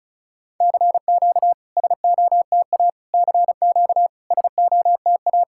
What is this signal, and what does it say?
Morse code, heard as a keyed tone, about 36 wpm, 700 Hz. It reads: CQ SOTA CQ SOTA
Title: SOTA QSO Practice 35wpm